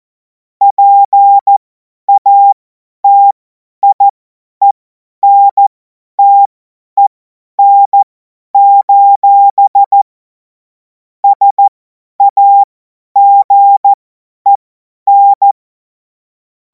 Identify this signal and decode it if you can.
Morse code, heard as a keyed tone, about 14 wpm, spaced as for 10 wpm, 800 Hz. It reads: PATIENTEN: SAGEN
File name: Morsekurs / Witze / Witz 18. August 2026 50 F70